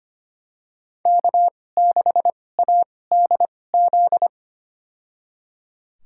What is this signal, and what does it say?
Morse code, heard as a keyed tone, about 25 wpm, spaced as for 18 wpm, 700 Hz. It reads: K6ADZ
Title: USA Call Signs - 2x Character Spacing - 25wpm